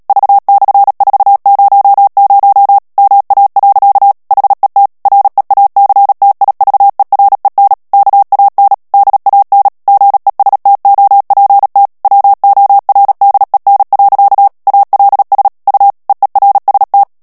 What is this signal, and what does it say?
Morse code, heard as a keyed tone, about 37 wpm, 800 Hz. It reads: V/400 MA. HET REACTIVEREN KAN DAN GESTOPT WORDEN. ALS U EERST
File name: morse mp3 / HOLLANDAIS texte clair 2 vitesse 35 mots minute